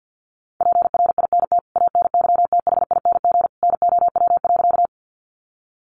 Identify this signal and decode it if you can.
Morse code, heard as a keyed tone, about 48 wpm, 700 Hz. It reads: PRINT ANYTHING NOW.
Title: Sentences from Top 1400 Words 50wpm